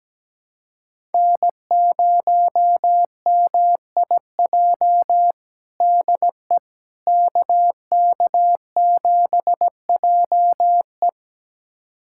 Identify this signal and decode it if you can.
Morse code, heard as a keyed tone, about 17 wpm, 700 Hz. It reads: N0MIJ DE KK7JE
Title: USA Callsign DE Callsign - 5 Characters - 17wpm